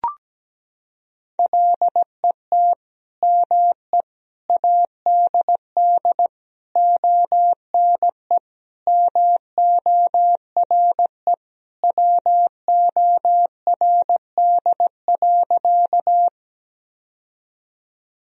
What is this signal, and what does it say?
Morse code, heard as a keyed tone, about 17 wpm, 700 Hz. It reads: LET ME ADD ONE MORE WORD.
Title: Sentences from Top 1000 Words - Review Part 1 - 17wpm